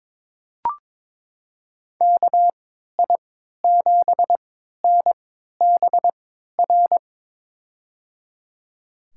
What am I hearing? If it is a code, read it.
Morse code, heard as a keyed tone, about 21 wpm, spaced as for 12 wpm, 700 Hz. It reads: KI7NBR
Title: USA Call Signs - 6 Characters - 3x Character Spacing - 22wpm